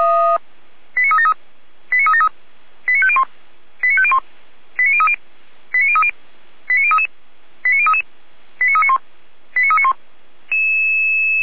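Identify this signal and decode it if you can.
Five-tone selective call (ZVEI), heard as quick runs of five tones, tones of about 70 ms. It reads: 89373 89373 89501 89501 89030 89030 89033 89033 89391 89391
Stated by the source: Five-tone ZVEI selective call received on an SDR.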